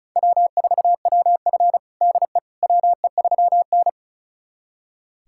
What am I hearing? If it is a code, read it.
Morse code, heard as a keyed tone, about 36 wpm, 700 Hz. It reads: W4WF DE WE3N
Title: USA Callsign DE Callsign - 4 Characters - 35wpm